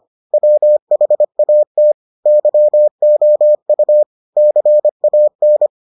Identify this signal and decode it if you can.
Morse code, heard as a keyed tone, about 25 wpm, 600 Hz. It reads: WHAT YOU CAN